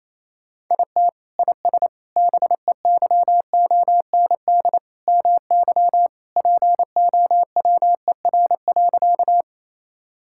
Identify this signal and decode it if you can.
Morse code, heard as a keyed tone, about 27 wpm, 700 Hz. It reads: IT IS BEYOND MY POWER.
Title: Sentences from Top 1000 Words - Review Part 6 - 28wpm